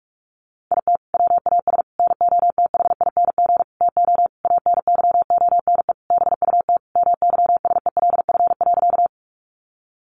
Morse code, heard as a keyed tone, about 44 wpm, 700 Hz. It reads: IT WAS NOTHING TO ANYONE BUT MYSELF.
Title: Sentences from Top 1300 Words 45wpm